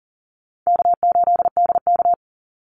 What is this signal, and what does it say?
Morse code, heard as a keyed tone, about 40 wpm, 700 Hz. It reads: K8DK
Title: Easy Call Signs Repeated 2x - 40wpm